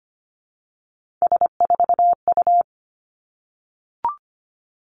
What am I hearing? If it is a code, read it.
Morse code, heard as a keyed tone, about 25 wpm, 700 Hz. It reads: S4U
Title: Three Challenging Letter-Number Combinations 25wpm